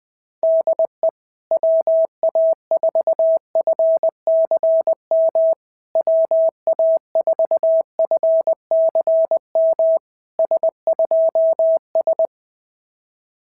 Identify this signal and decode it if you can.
Morse code, heard as a keyed tone, about 20 wpm, 650 Hz. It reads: DE WA4FCM WA4FCM S2S
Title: SOTA QSO Practice 20wpm